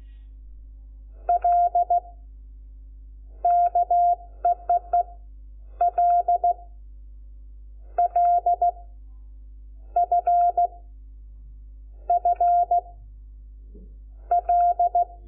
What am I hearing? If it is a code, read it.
Morse code, heard as a keyed tone, about 16 wpm, 700 Hz. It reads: L KEEE L L F F L
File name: Lecciones / CLASE Nº5